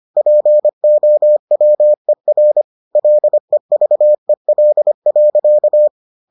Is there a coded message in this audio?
Yes: Morse code from 0.2 to 5.9 s, heard as a keyed tone, about 25 wpm, 600 Hz. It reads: POWER LEVEL.